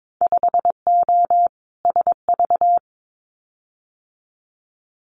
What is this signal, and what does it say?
Morse code, heard as a keyed tone, about 22 wpm, 700 Hz. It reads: 5O SV
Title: Winter Field Day Exchanges Encore 22wpm